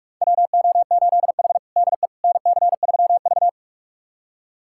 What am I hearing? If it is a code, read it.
Morse code, heard as a keyed tone, about 45 wpm, 700 Hz. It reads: WO8H DE NC3V
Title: USA Callsign DE Callsign - 4 Characters - 45wpm